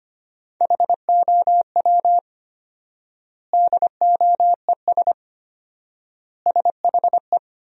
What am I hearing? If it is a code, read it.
Morse code, heard as a keyed tone, about 25 wpm, 700 Hz. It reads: HOW DOES SHE
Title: 4x Word-Spacing - Sentences from Top 200 Words - 25wpm